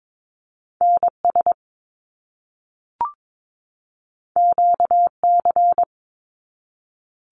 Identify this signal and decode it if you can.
Morse code, heard as a keyed tone, about 22 wpm, 700 Hz. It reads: NS QC
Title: Canadian Provinces and Territories 22wpm